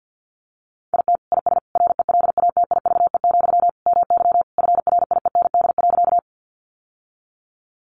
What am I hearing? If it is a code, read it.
Morse code, heard as a keyed tone, about 50 wpm, 700 Hz. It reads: IT IS RELATIVE, MY FRIEND.
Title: Sentences from Top 1400 Words 50wpm